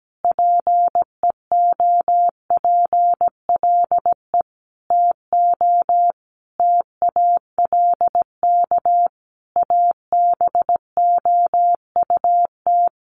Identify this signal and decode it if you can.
Morse code, heard as a keyed tone, about 17 wpm, 700 Hz. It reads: PEOPLE TO TALK ABOUT
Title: Sentences from Top 1400 Words 17wpm